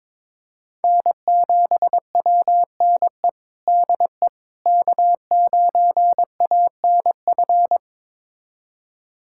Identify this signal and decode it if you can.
Morse code, heard as a keyed tone, about 22 wpm, 700 Hz. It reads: N7WNE DE K9ANF
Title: USA Callsign DE Callsign - 5 Characters - 22wpm